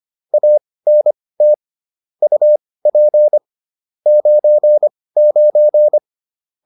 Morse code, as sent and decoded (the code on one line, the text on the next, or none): .- -. - / ..- .--. / ----. ----.
ANT UP 99